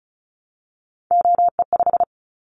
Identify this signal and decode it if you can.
Morse code, heard as a keyed tone, about 36 wpm, 700 Hz. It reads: OE5
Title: Callsign Prefixes for Top 50 DXCC - 35wpm